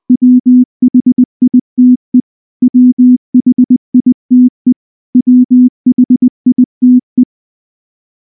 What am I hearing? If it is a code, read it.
Morse code, heard as a keyed tone, about 20 wpm, 250 Hz. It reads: WHITE WHITE WHITE
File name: morse / morse-training-20WPM-7